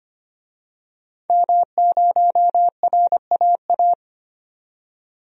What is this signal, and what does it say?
Morse code, heard as a keyed tone, about 25 wpm, 700 Hz. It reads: M0RAA
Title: DXCC-100 Call Signs - 5 Characters - 25wpm